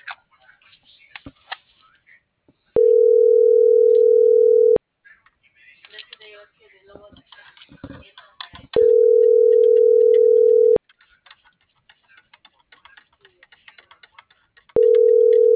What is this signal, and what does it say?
Ringback tone (North America), 440 + 480 Hz.